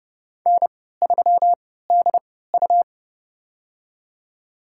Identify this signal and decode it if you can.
Morse code, heard as a keyed tone, about 32 wpm, spaced as for 17 wpm, 700 Hz. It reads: N3DU
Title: USA Call Signs - 4 Characters - 3x Character Spacing - 30wpm